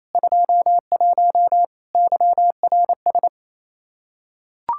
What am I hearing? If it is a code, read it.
Morse code, heard as a keyed tone, about 28 wpm, 700 Hz. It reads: 21 YRS
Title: Practice QSO - Exchange 3 - More Info - 28wpm